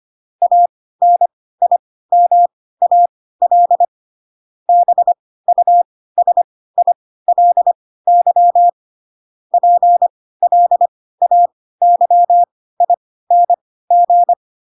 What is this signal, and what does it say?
Morse code, heard as a keyed tone, about 25 wpm, spaced as for 16 wpm, 700 Hz. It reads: ANIMAL BUSILY PLAYING
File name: Pinocchio / Pinocchio ch7 10 25 ik1hge